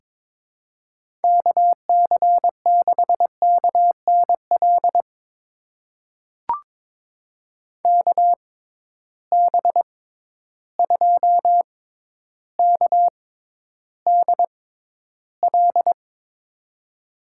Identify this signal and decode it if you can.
Morse code, heard as a keyed tone, about 22 wpm, spaced as for 11 wpm, 700 Hz. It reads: KC6KNL KB2KDL
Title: USA Call Signs - 6 Characters - 6x Character Spacing - 22wpm